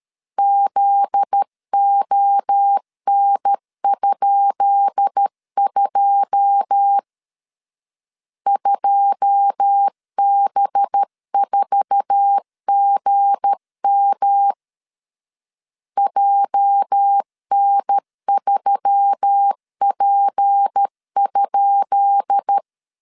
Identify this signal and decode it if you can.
Morse code, heard as a keyed tone, about 13 wpm, 800 Hz. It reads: ZON?2 2B4GM JN3P?
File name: LEZIONI / 10 SIGNS / 10B INT 12wpm 2414